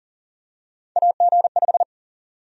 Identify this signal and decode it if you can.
Morse code, heard as a keyed tone, about 40 wpm, 700 Hz. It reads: AG5
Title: USA Call Sign Prefixes - 40wpm